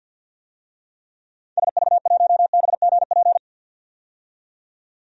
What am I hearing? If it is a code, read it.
Morse code, heard as a keyed tone, about 45 wpm, 700 Hz. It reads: SV1BGP
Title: DXCC-100 Call Signs - 6 Characters - 50wpm